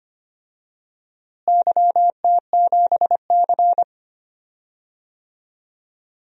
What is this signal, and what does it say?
Morse code, heard as a keyed tone, about 25 wpm, 700 Hz. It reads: YT7C
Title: DXCC-100 Call Signs - 4 Characters - Repeated 3x - 25wpm